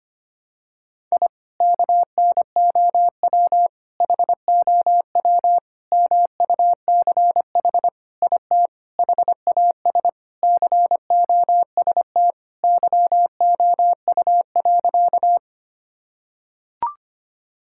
Morse code, heard as a keyed tone, about 25 wpm, 700 Hz. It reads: I KNOW HOW MUCH IT HAS COST YOU.
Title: Sentences from Top 1000 Words - Review Part 5 - 25wpm